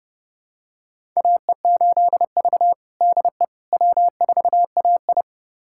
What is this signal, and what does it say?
Morse code, heard as a keyed tone, about 30 wpm, 700 Hz. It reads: AE8V DE W4AI
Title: USA Callsign DE Callsign - 4 Characters - 30wpm